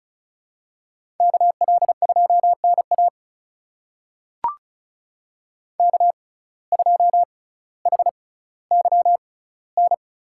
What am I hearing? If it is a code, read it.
Morse code, heard as a keyed tone, about 34 wpm, spaced as for 18 wpm, 700 Hz. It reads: KL2NA K2HYN
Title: USA Call Signs - 5 Characters - 6x Character Spacing - 35wpm